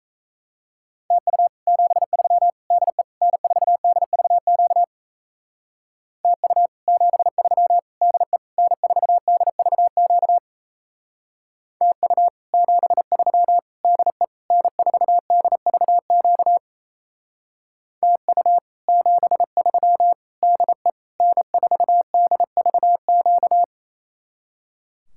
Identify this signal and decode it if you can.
Morse code, heard as a keyed tone, about 36 wpm, 700 Hz. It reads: TU 73 DE N4DVQ TU 73 DE N4DVQ TU 73 DE N4DVQ TU 73 DE N4DVQ